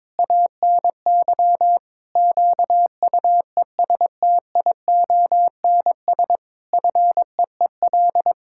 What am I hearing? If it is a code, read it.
Morse code, heard as a keyed tone, about 22 wpm, 700 Hz. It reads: ANY QUESTIONS FEEL
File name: practice / 22wpm / Sets of 8 Words - 22wpm